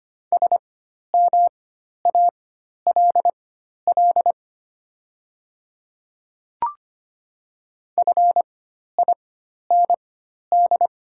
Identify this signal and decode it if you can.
Morse code, heard as a keyed tone, about 25 wpm, spaced as for 12 wpm, 700 Hz. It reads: SMALL FIND
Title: 4x Character Spacing - Top 300 Words - 25wpm